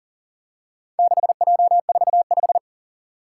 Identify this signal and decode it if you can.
Morse code, heard as a keyed tone, about 37 wpm, 700 Hz. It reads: 6J45
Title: Four Challenging Letter-Number Combinations 40wpm